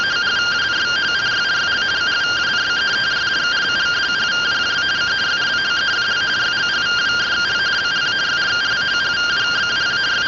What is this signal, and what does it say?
Two-tone teleprinter signal, about 170 Hz shift.